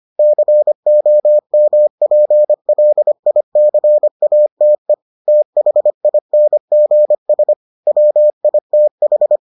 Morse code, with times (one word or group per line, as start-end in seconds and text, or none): 0.2-4.9 COMPLICATE
5.3-7.5 THINGS
7.9-9.4 WITH